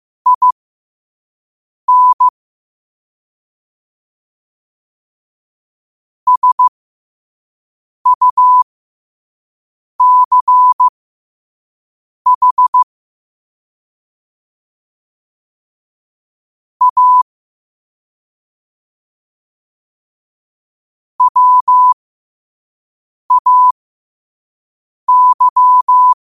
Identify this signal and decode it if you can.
Morse code, heard as a keyed tone, about 15 wpm, spaced as for 5 wpm, 1000 Hz. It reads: IN SUCH A WAY
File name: qotdfiles / 5wpm